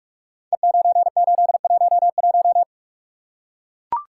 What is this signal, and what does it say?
Morse code, heard as a keyed tone, about 44 wpm, 700 Hz. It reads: E9811
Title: Code Groups - 5 Letter:Numbers - 45wpm